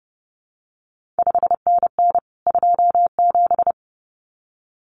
Morse code, heard as a keyed tone, about 31 wpm, 700 Hz. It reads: 5NN 27
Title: CQ WW DX Contest Exchange Encore - 30wpm